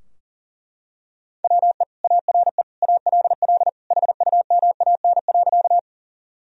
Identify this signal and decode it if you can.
Morse code, heard as a keyed tone, about 37 wpm, 700 Hz. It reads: WE ARE ALL HUMAN.
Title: Sentences from Top 1000 Words - Review Part 3 - 40wpm